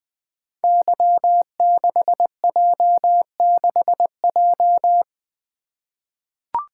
Morse code, heard as a keyed tone, about 21 wpm, 700 Hz. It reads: Y6J6J
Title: Code Groups - 5 Letter:Numbers - 20wpm